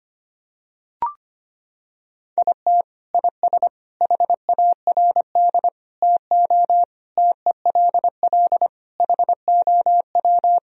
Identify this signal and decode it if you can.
Morse code, heard as a keyed tone, about 25 wpm, 700 Hz. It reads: IT IS HARD TO TELL HOW
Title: Sentences from Top 1000 Words - Review Part 3 - 25wpm